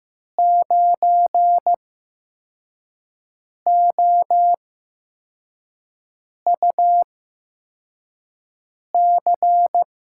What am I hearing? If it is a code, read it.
Morse code, heard as a keyed tone, about 15 wpm, spaced as for 4 wpm, 700 Hz. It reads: 9OUC
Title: USA Call Signs - 8x Character Spacing - 15wpm